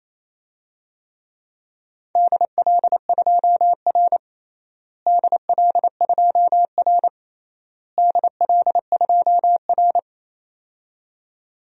Morse code, heard as a keyed tone, about 27 wpm, 700 Hz. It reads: DL2R DL2R DL2R
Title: DXCC-100 Call Signs - 4 Characters - Repeated 3x - 28wpm